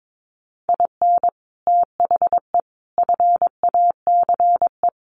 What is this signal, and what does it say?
Morse code, heard as a keyed tone, about 22 wpm, 700 Hz. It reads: IN THE FACE
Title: Sets of 4 Words 22wpm